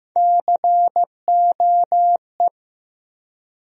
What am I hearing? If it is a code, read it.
Morse code, heard as a keyed tone, about 15 wpm, 700 Hz. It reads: COE
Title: USA Call Sign Postfixes - 3 Letters 15wpm